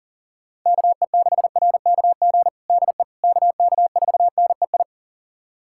Morse code, heard as a keyed tone, about 40 wpm, 700 Hz. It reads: KE6RKG DE KK4NEI
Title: USA Callsign DE Callsign 40wpm